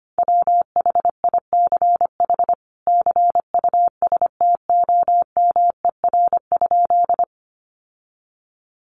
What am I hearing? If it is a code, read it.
Morse code, heard as a keyed tone, about 25 wpm, 700 Hz. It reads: WHICH CUSTOMER?
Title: Sentences from Top 1100 Words 25wpm